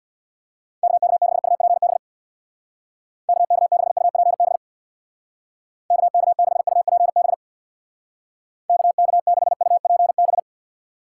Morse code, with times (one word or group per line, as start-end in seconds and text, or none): 0.8-2.0 KK6UPB
3.3-4.6 KK6UPB
5.9-7.3 KK6UPB
8.7-10.4 KK6UPB